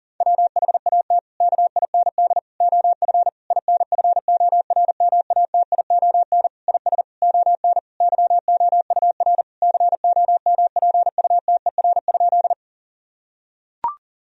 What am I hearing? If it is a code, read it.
Morse code, heard as a keyed tone, about 37 wpm, 700 Hz. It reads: WHAT KIND OF INFORMATION IS ON YOUR COMPUTER?
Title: Sentences from Top 1000 Words - Review Part 3 - 40wpm